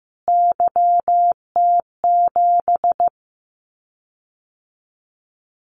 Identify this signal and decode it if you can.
Morse code, heard as a keyed tone, about 15 wpm, 700 Hz. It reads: YT7
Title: Callsign Prefixes for Top 25 DXCC - Repeated 3x - 15wpm